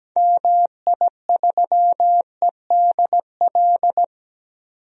Morse code, heard as a keyed tone, about 17 wpm, 700 Hz. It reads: MI3EDL
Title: DXCC-100 Call Signs - 6 Characters - 17wpm